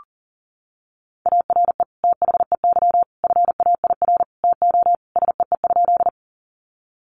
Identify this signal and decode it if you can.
Morse code, heard as a keyed tone, about 40 wpm, 700 Hz. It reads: ARE THEY FAIR TO SEE?
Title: Sentences from Top 1300 Words 40wpm